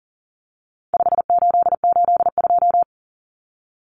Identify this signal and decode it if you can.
Morse code, heard as a keyed tone, about 40 wpm, 700 Hz. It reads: 5882
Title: Numbers - 4-Digit 40wpm